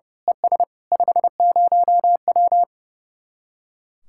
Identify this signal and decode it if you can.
Morse code, heard as a keyed tone, about 28 wpm, 700 Hz. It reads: ES 50W
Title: Speed-Racing Common QSO Phrases Encore - 30wpm